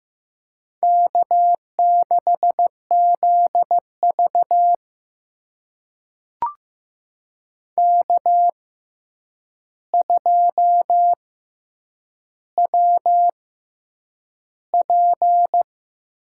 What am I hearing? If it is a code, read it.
Morse code, heard as a keyed tone, about 15 wpm, spaced as for 8 wpm, 700 Hz. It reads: K6ZV K2WP
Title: USA Call Signs - 4 Characters - 6x Character Spacing - 15wpm